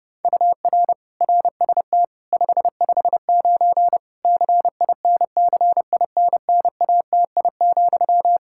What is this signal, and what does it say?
Morse code, heard as a keyed tone, about 32 wpm, 700 Hz. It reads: UR RST 559 CINCINNATI,
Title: Practice QSO - Exchange 1 - Beginning - 30wpm